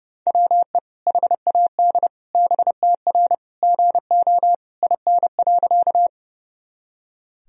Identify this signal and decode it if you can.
Morse code, heard as a keyed tone, about 32 wpm, 700 Hz. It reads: WE HAD BTR GO IN.
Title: Sentences from Top 400 Words and Abbreviations 30wpm